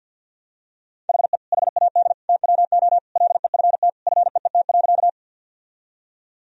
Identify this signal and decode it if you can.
Morse code, heard as a keyed tone, about 47 wpm, 700 Hz. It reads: HE HAD TWO LEFT FEET.
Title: Sentences from Top 1000 Words - Review Part 1 - 50wpm